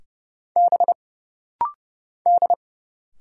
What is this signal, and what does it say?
Morse code, heard as a keyed tone, about 29 wpm, 700 Hz. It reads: B D
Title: Single Letter-Number - Rapid-Fire 30wpm